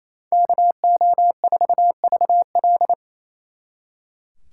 Morse code, heard as a keyed tone, about 34 wpm, 700 Hz. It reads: KO4VL
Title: Speed Racing - USA 2x2 Call Signs 28wpm